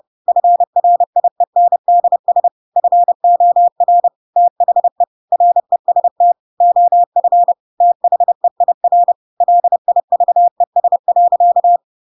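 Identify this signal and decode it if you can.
Morse code, heard as a keyed tone, about 30 wpm, 700 Hz. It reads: FRIENDS FOR THE REST OF THEIR LIVES.
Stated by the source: Chapter 2 of The Adventures of Pinocchio read as a book in Morse code (CW) at 30 wpm, by ik1hge.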